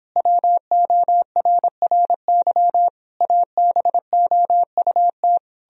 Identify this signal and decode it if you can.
Morse code, heard as a keyed tone, about 29 wpm, 700 Hz. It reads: WORRY ABOUT